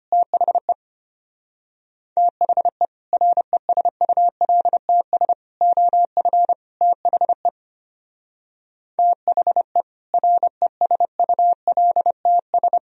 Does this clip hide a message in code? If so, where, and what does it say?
Yes: Morse code from 0.1 to 12.8 s, heard as a keyed tone, about 32 wpm, 700 Hz. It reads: THE THE RESULTS OF THE THE RESULTS